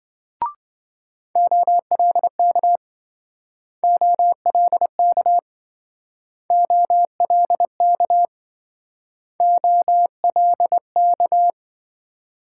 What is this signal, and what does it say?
Morse code, heard as a keyed tone, about 26 wpm, 700 Hz. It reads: OLK OLK OLK OLK